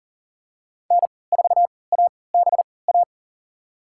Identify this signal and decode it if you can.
Morse code, heard as a keyed tone, about 40 wpm, spaced as for 23 wpm, 690 Hz. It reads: N4ABA
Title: USA Call Signs - 5 Characters - 3x Character Spacing - 40wpm